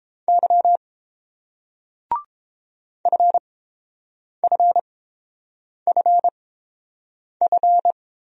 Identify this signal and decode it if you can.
Morse code, heard as a keyed tone, about 30 wpm, spaced as for 7 wpm, 700 Hz. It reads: Y FFFF